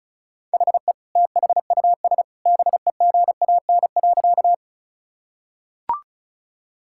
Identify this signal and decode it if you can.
Morse code, heard as a keyed tone, about 36 wpm, 690 Hz. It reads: HE THUS BEGAN.
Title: Sentences from Top 1000 Words - Review Part 5 - 35wpm